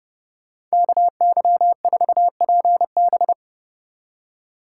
Morse code, heard as a keyed tone, about 33 wpm, 700 Hz. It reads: KY4PB